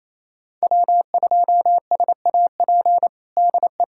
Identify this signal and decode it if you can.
Morse code, heard as a keyed tone, about 28 wpm, 700 Hz. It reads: W2SAP DE
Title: USA Callsign DE Callsign - 5 Characters - 28wpm